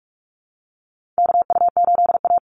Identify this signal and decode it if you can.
Morse code, heard as a keyed tone, about 44 wpm, 700 Hz. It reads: KU8A
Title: USA Call Signs 45wpm